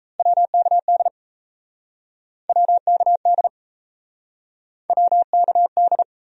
Morse code, heard as a keyed tone, about 38 wpm, 700 Hz. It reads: WKD WKD WKD